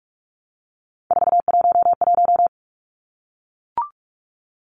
Morse code, heard as a keyed tone, about 44 wpm, 700 Hz. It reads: 411
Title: Numbers - 3-Digit 45wpm